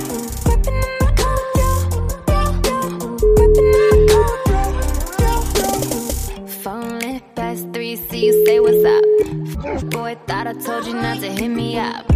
Ringback tone (Europe, 425 Hz), with noise.